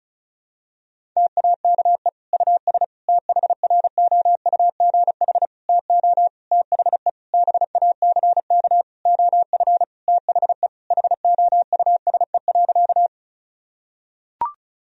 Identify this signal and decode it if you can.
Morse code, heard as a keyed tone, about 36 wpm, 700 Hz. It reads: TAKE US THROUGH TO THE BACK OF THE HOUSE.
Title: Sentences from Top 1000 Words - Review Part 1 - 35wpm